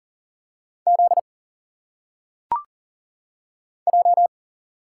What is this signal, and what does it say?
Morse code, heard as a keyed tone, about 40 wpm, 700 Hz. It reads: Z J
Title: Single Letters - Phonetic Pronunciation 40wpm